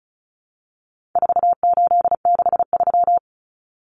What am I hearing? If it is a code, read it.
Morse code, heard as a keyed tone, about 34 wpm, 700 Hz. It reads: 4863
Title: Common Cut Numbers - 4-Digit 35wpm